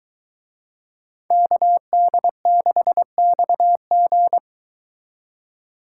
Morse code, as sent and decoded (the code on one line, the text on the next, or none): -.- -.. -.... -..- --.
KD6XG